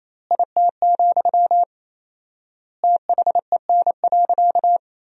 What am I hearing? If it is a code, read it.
Morse code, heard as a keyed tone, about 28 wpm, 700 Hz. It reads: IT, THEN.
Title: 4x Word-Spacing - Sentences from Top 200 Words - 28wpm